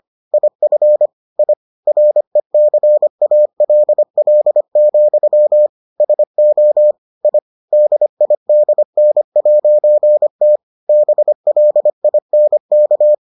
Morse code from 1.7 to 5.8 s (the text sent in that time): RECALL,